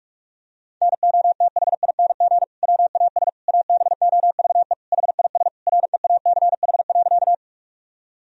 Morse code, heard as a keyed tone, about 46 wpm, 700 Hz. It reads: NOTHING WAS ABOVE HIS REACH.
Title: Sentences from Top 1000 Words - Review Part 2 - 45wpm